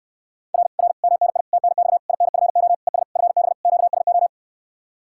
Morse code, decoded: R R MNI TNX INFO ES FB QSO

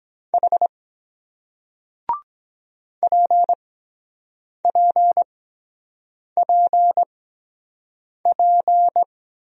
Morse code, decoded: H PPPP

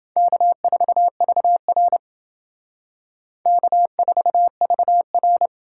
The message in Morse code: K4VR K4VR

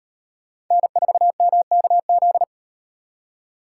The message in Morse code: N4MKZ

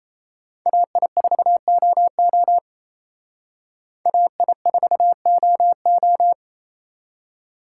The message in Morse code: AI4OO AI4OO

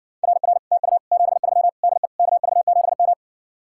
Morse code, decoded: RR TU 73 DE KV7M

